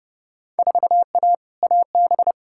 Morse code, heard as a keyed tone, about 32 wpm, 700 Hz. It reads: 4A AB
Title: Field Day Exchanges Encore - 30wpm